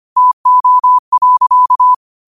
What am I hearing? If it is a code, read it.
Morse code, heard as a keyed tone, about 25 wpm, 1000 Hz. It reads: TO.